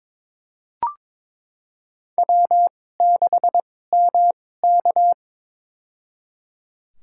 Morse code, heard as a keyed tone, about 22 wpm, spaced as for 16 wpm, 700 Hz. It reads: W6MK
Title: USA Call Signs - 4 Characters - 2x Character Spacing - 22wpm